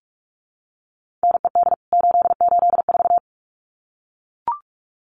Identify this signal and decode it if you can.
Morse code, heard as a keyed tone, about 44 wpm, 700 Hz. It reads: NED 884
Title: CWT Exchange 45wpm